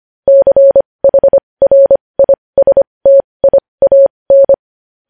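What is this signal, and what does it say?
Morse code, heard as a keyed tone, about 25 wpm, spaced as for 20 wpm, 550 Hz. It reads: CHRISTIAN
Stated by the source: News headlines sent in Morse code at 20.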